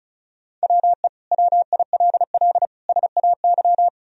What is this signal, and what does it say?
Morse code, heard as a keyed tone, about 34 wpm, 700 Hz. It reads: WE WILL SAY
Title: Sentences from Top 200 Words and Abbreviations 35wpm